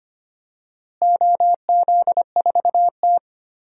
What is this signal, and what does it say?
Morse code, heard as a keyed tone, about 25 wpm, 700 Hz. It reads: OZ4T
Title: DXCC-100 Call Signs - 4 Characters - Repeated 2x - 25wpm